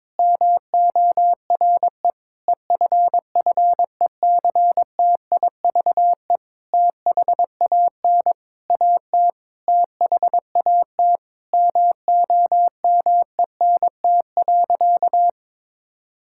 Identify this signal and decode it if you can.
Morse code, heard as a keyed tone, about 22 wpm, 700 Hz. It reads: MORE EFFECTIVE THAN AT THAT MOMENT.
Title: Sentences from Top 1000 Words - Review Part 6 - 22wpm